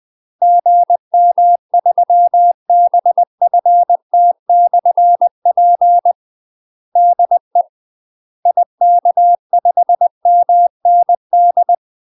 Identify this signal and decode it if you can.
Morse code, heard as a keyed tone, about 20 wpm, 700 Hz. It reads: GM3BFT/P DE IK5MND
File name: UK MASTER QSOs TRAINING / 20WPM / UK MASTER QSOs 0060